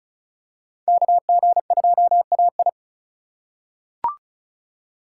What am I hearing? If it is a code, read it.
Morse code, heard as a keyed tone, about 36 wpm, 700 Hz. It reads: KG2AI
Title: USA Call Signs - 5 Characters - 4x Character Spacing - 35wpm